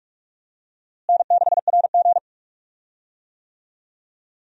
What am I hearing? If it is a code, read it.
Morse code, heard as a keyed tone, about 45 wpm, 700 Hz. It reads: N6RG
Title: USA Call Signs - 4 Characters - Repeated 2x - 45wpm